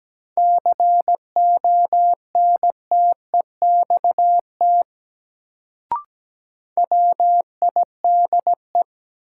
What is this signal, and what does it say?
Morse code, heard as a keyed tone, about 17 wpm, 700 Hz. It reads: CONTEXT WIDE
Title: Sets of 4 Words - New Word Drill - 17wpm